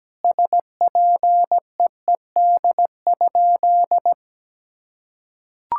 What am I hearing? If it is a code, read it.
Morse code, heard as a keyed tone, about 17 wpm, 700 Hz. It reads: SPEED?
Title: Sentences from Top 1000 Words - Review Part 5 - 17wpm